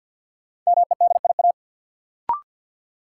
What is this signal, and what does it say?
Morse code, heard as a keyed tone, about 51 wpm, 700 Hz. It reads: MEDIA
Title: Sentences from Top 1000 Words - Review Part 4 - New Word Drill - 50wpm